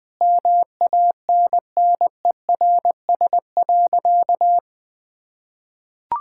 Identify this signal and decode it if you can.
Morse code, heard as a keyed tone, about 19 wpm, 700 Hz. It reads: MANNERS.
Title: Difficult Sentences with Hints - 20wpm